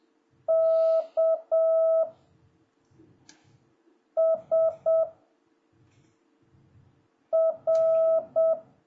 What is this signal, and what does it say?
Morse code, heard as a keyed tone, about 7 wpm, 660 Hz. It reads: K S R
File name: jh8cbh / 2026cw